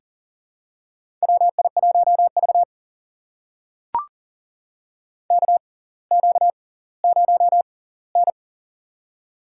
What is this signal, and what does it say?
Morse code, heard as a keyed tone, about 40 wpm, spaced as for 20 wpm, 700 Hz. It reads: WI1V KQ0N
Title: USA Call Signs - 6x Character Spacing - 40wpm